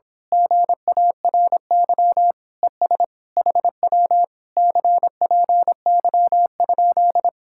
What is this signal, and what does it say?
Morse code, heard as a keyed tone, about 26 wpm, 700 Hz. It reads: GARY ES HW CPY?
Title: Speed-Racing Common QSO Phrases Encore - 22wpm